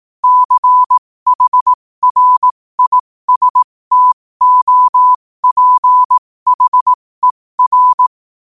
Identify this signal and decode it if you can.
Morse code, heard as a keyed tone, about 18 wpm, spaced as for 15 wpm, 1000 Hz. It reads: CHRISTOPHER
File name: qotdfiles / 15wpm